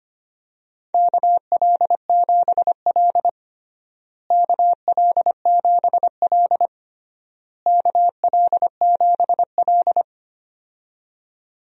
Morse code, heard as a keyed tone, about 25 wpm, 700 Hz. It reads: KL7L KL7L KL7L
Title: DXCC-100 Call Signs - 4 Characters - Repeated 3x - 25wpm